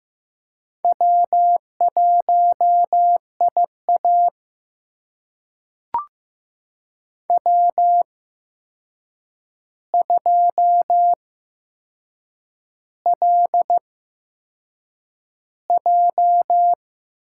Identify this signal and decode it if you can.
Morse code, heard as a keyed tone, about 15 wpm, spaced as for 6 wpm, 700 Hz. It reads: W1IA W2LJ
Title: USA Call Signs - 4 Characters - 8x Character Spacing - 15wpm